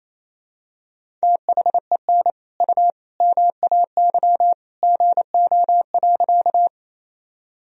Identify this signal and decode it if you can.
Morse code, heard as a keyed tone, about 28 wpm, 700 Hz. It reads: THEN U MAY GO.
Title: Sentences from Top 100 Words and Abbreviations 28wpm